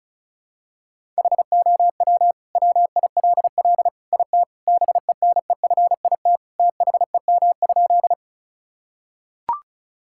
Morse code, heard as a keyed tone, about 36 wpm, 700 Hz. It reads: HOW WILL IT BENEFIT THEM?
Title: Sentences from Top 1000 Words - Review Part 6 - 35wpm